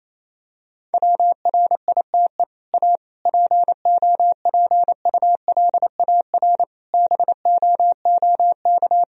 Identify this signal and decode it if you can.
Morse code, heard as a keyed tone, about 28 wpm, 700 Hz. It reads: WRITE A POPULAR BOOK
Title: Sentences from Top 1000 Words - Review Part 3 - 28wpm